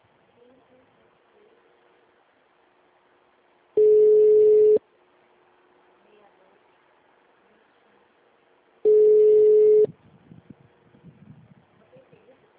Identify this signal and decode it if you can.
Ringback tone (Europe, 425 Hz), with speech.